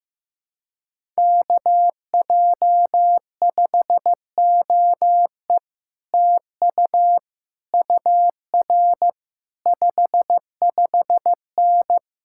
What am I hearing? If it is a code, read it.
Morse code, heard as a keyed tone, about 15 wpm, 700 Hz. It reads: KJ5OE TU UR 55N
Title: Speed-Racing Common QSO Phrases Encore - 15wpm